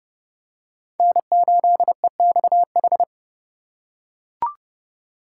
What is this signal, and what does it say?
Morse code, heard as a keyed tone, about 30 wpm, 700 Hz. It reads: N8EXH
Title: USA Call Signs - 5 Characters - 2x Character Spacing - 30wpm